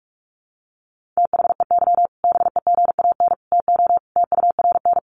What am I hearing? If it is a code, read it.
Morse code, heard as a keyed tone, about 44 wpm, 700 Hz. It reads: THEY BEGAN TO TURN
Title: Sentences from Top 1100 Words 45wpm